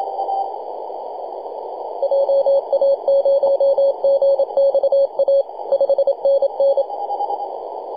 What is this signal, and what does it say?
Morse code, heard as a keyed tone, about 27 wpm, 550 Hz, with noise. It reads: JA0GXA 5NN
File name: posts / 2015-nov-21-0614z-v63yy-24-905mhz-cw